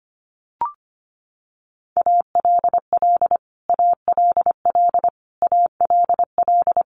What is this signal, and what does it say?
Morse code, heard as a keyed tone, about 25 wpm, 700 Hz. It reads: ALL ALL ALL
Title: Top 300 Words Repeated 25wpm